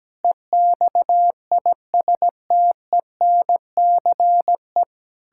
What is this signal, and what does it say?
Morse code, heard as a keyed tone, about 17 wpm, 700 Hz. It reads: EXISTENCE
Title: Pangram Challenge v2 - 17wpm